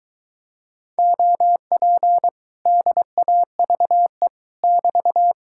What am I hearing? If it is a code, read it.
Morse code, heard as a keyed tone, about 23 wpm, 700 Hz. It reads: OP DAVE <BT>